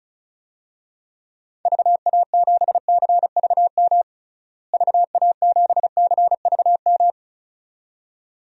Morse code, heard as a keyed tone, about 36 wpm, 700 Hz. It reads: VA7CVM VA7CVM
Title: DXCC-100 Call Signs - 6 Characters - Repeated 2x - 35wpm